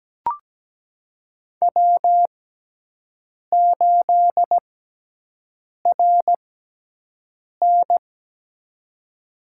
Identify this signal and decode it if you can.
Morse code, heard as a keyed tone, about 17 wpm, spaced as for 6 wpm, 700 Hz. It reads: W8RN
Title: USA Call Signs - 4 Characters - 6x Character Spacing - 17wpm